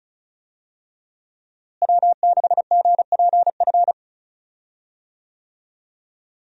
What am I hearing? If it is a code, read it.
Morse code, heard as a keyed tone, about 36 wpm, 700 Hz. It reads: W6GPF
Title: USA Call Signs - 5 Characters - Repeated 3x - 35wpm